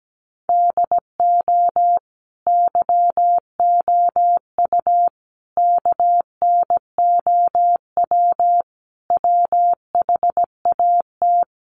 Morse code, heard as a keyed tone, about 17 wpm, 700 Hz. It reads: DO YOU KNOW WHAT
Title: Sentences from Top 300 Words 17wpm